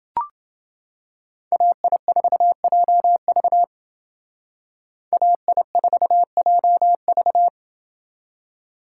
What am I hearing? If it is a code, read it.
Morse code, heard as a keyed tone, about 28 wpm, 700 Hz. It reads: AI4JV AI4JV